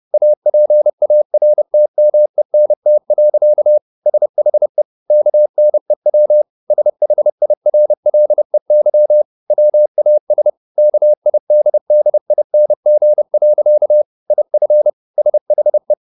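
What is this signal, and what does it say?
Morse code, heard as a keyed tone, about 30 wpm, 600 Hz. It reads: APARTMENT. SHE KNEW SHIRLEY WAS KIDDING. IF SHE
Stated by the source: Morse code at 30 wpm, the novel Never by Ken Follett in English, from CW med Gnister.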